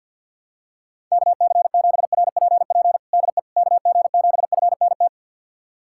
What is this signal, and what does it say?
Morse code, heard as a keyed tone, about 46 wpm, 700 Hz. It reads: KC7RPP DE KG7FNT
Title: USA Callsign DE Callsign - 6 Characters - 50wpm